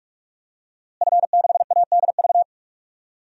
Morse code, heard as a keyed tone, about 46 wpm, 700 Hz. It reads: F6ADV
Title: DXCC-100 Call Signs - 5 Characters - Repeated 3x - 45wpm